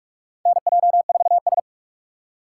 Morse code, heard as a keyed tone, about 48 wpm, 700 Hz. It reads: NJ4S
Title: USA Call Signs - 3x Character Spacing - 45wpm